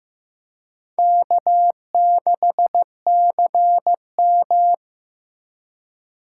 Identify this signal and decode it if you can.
Morse code, heard as a keyed tone, about 15 wpm, 700 Hz. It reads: K6CM
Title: USA Call Signs - 4 Characters - 3x Character Spacing - 15wpm